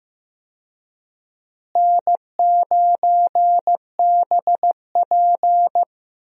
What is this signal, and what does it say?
Morse code, heard as a keyed tone, about 15 wpm, 700 Hz. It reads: N9BP
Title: USA Call Signs - 4 Characters - Repeated 2x - 15wpm